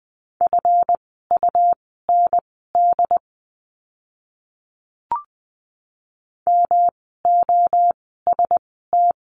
Morse code, heard as a keyed tone, about 20 wpm, spaced as for 14 wpm, 700 Hz. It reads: FUND MOST
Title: 139 Lesson 23f - Letter f - Words 20wpm